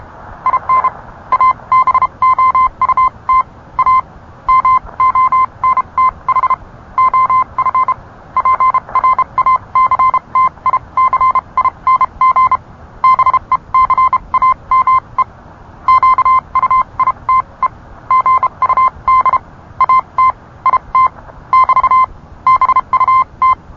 Morse code, heard as a keyed tone, about 31 wpm, spaced as for 26 wpm, 1030 Hz. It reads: IN ABOUT A MONTH OF PRACTICING BECAME QUITE GUD AT IT <BT> BUT